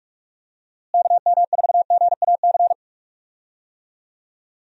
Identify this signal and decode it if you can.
Morse code, heard as a keyed tone, about 44 wpm, 700 Hz. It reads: KM4GAC
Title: USA Call Signs - 6 Characters - Repeated 3x - 45wpm